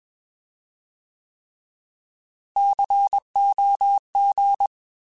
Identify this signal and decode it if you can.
Morse code, heard as a keyed tone, about 21 wpm, 800 Hz. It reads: COG